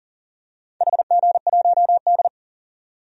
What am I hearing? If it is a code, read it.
Morse code, heard as a keyed tone, about 40 wpm, 700 Hz. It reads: HG1D
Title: DXCC-100 Call Signs - 4 Characters - 40wpm